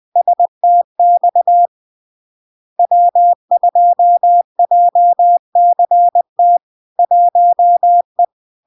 Morse code, heard as a keyed tone, about 20 wpm, 700 Hz. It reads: STX W2JCT 1E
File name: LARK-FD-2022-Log-in-CW-20wpm-700-Hz